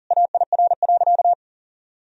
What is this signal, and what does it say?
Morse code, heard as a keyed tone, about 40 wpm, 700 Hz. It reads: AIR.
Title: Sentences from Top 1000 Words - Review Part 1 - 40wpm